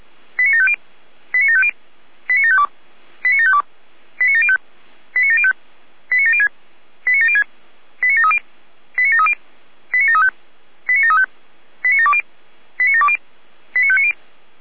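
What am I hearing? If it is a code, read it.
Five-tone selective call (ZVEI), heard as quick runs of five tones, tones of about 70 ms. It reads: 89750 89750 89752 89752 89705 89705 89706 89706 89730 89730 89735 89735 89720 89720 89590
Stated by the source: ZVEI five-tone selective call received on an SDR.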